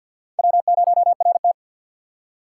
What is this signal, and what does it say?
Morse code, heard as a keyed tone, about 52 wpm, 700 Hz. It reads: W0RT
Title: USA Call Signs - 6x Character Spacing - 50wpm